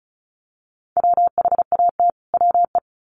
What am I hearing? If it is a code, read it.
Morse code, heard as a keyed tone, about 34 wpm, 700 Hz. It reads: WHAT WE
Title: Sets of 2 Words 35wpm